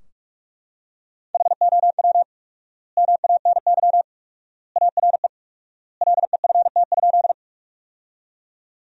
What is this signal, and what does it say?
Morse code, heard as a keyed tone, about 43 wpm, 700 Hz. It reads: HOW MANY ARE LEFT?
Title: 4x Word-Spacing - Sentences from Top 300 Words - 45wpm